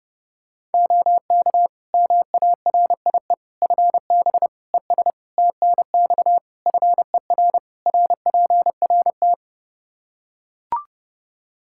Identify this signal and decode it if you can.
Morse code, heard as a keyed tone, about 32 wpm, 700 Hz. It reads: OK MARIE FB ES TNX FER RPRT
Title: Practice QSO - Exchange 2 - Info - 30wpm